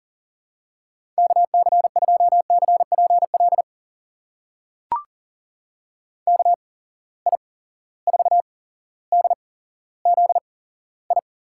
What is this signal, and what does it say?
Morse code, heard as a keyed tone, about 40 wpm, spaced as for 17 wpm, 700 Hz. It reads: KC2CPL KI4DZI
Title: USA Call Signs - 8x Character Spacing - 40wpm